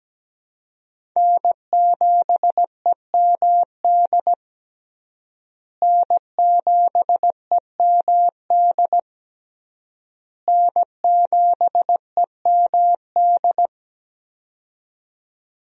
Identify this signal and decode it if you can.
Morse code, heard as a keyed tone, about 17 wpm, 700 Hz. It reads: N7EMD N7EMD N7EMD
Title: USA Call Signs - 5 Characters - Repeated 3x - 17wpm